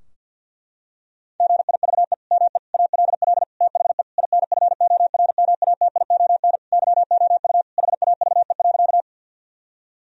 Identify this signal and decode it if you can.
Morse code, heard as a keyed tone, about 52 wpm, 700 Hz. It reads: GIVE ME ALL THE INFORMATION YOU HAVE.
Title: Sentences from Top 1000 Words - Review Part 3 - 50wpm